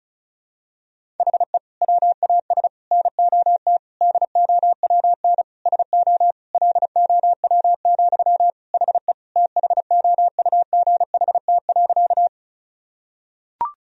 Morse code, heard as a keyed tone, about 34 wpm, 700 Hz. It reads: HE WAS NOT DOWN SO LOW, HE THOUGHT.
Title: Sentences from Top 1000 Words - Review Part 1 - 35wpm